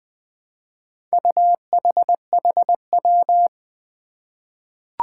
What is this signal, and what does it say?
Morse code, heard as a keyed tone, about 20 wpm, 700 Hz. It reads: UHHW
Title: Four Challenging Letter-Number Combinations 20wpm